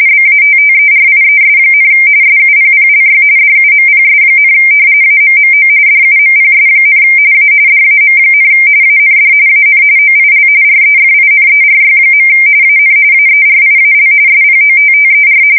RTTY, about 45.45 baud. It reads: BATTERIES, SUPPLY DUMPS, AMMO DUMPS, A DAM, AND AN OIL REFINERY. AN AERIAL SPOTTER
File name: fsk170 / battleshipnj